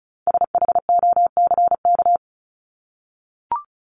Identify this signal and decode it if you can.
Morse code, heard as a keyed tone, about 34 wpm, 700 Hz. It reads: SHOCK
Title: Top 1751-2000 Words 35wpm